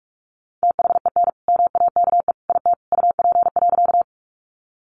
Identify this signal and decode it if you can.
Morse code, heard as a keyed tone, about 44 wpm, 700 Hz. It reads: THEN MAKE IT UP.
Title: Sentences from Top 100 Words 45wpm